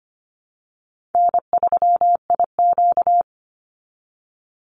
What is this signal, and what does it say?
Morse code, heard as a keyed tone, about 25 wpm, 700 Hz. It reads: N3IQ
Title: Easy Call Signs Repeated 2x - 25wpm